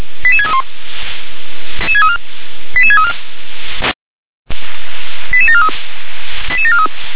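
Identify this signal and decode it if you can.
Five-tone selective call (ZVEI), heard as quick runs of five tones, tones of about 80 ms. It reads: 88631 88634 88743 88743
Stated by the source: ZVEI five-tone selective call 88743, received by SDR.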